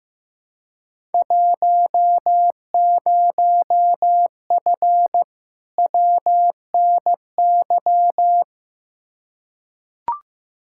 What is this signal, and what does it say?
Morse code, heard as a keyed tone, about 15 wpm, 700 Hz. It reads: 10F WNY
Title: Field Day Exchanges Encore - 15wpm